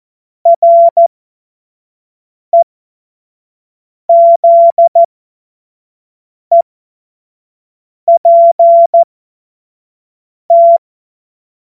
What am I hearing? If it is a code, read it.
Morse code, heard as a keyed tone, about 14 wpm, spaced as for 5 wpm, 680 Hz. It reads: REZEPT
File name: Witz-08-Juli-2025-25-F70